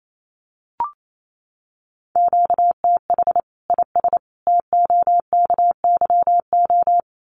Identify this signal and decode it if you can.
Morse code, heard as a keyed tone, about 28 wpm, 700 Hz. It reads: QTH IS TOKYO
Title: Common QSO Phrases 28wpm